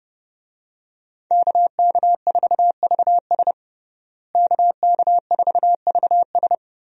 Morse code, heard as a keyed tone, about 32 wpm, 700 Hz. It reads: KK4VS KK4VS
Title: USA Call Signs - 5 Characters - Repeated 3x - 30wpm